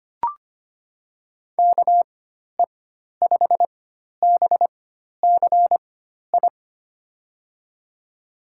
Morse code, heard as a keyed tone, about 25 wpm, spaced as for 12 wpm, 700 Hz. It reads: KE5BCI
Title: USA Call Signs - 6 Characters - 4x Character Spacing - 25wpm